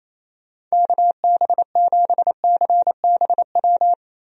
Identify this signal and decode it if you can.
Morse code, heard as a keyed tone, about 28 wpm, 700 Hz. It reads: KB7CBW
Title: USA Call Signs - 6 Characters - Repeated 3x - 28wpm